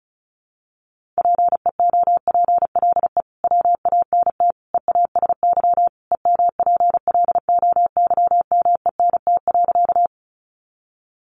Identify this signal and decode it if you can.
Morse code, heard as a keyed tone, about 34 wpm, 700 Hz. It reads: PEOPLE WANT EASY EMPLOYMENT.
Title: Sentences from Top 1500 Words 35wpm